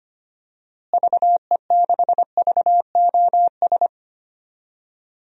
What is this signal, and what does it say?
Morse code, heard as a keyed tone, about 25 wpm, 700 Hz. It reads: VE6VOS
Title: DXCC-100 Call Signs - 6 Characters - Repeated 2x - 25wpm